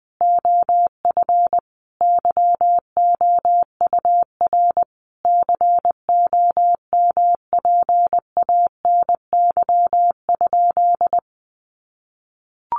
Morse code, heard as a keyed tone, about 20 wpm, 700 Hz. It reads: OF YOUR COMPANY?
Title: Sentences from Top 600 Words 20wpm